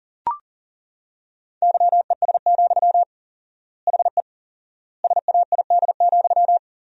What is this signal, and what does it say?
Morse code, heard as a keyed tone, about 40 wpm, 700 Hz. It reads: YES, HE SAID,
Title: 4x Word-Spacing - Sentences from Top 500 Words - 40wpm